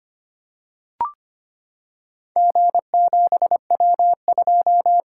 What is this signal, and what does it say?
Morse code, heard as a keyed tone, about 25 wpm, 700 Hz. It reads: G7W2
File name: practice / 25wpm / Four Letter-Number Combinations - 25wpm